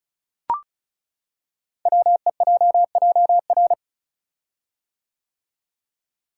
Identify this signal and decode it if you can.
Morse code, heard as a keyed tone, about 36 wpm, 700 Hz. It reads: WEJJR
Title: Code Groups - 5 Letters - 35wpm